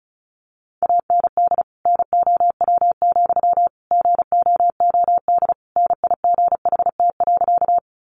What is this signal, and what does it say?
Morse code, heard as a keyed tone, about 34 wpm, 700 Hz. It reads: AND NOW, GOOD NIGHT.
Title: Sentences from Top 300 Words 35wpm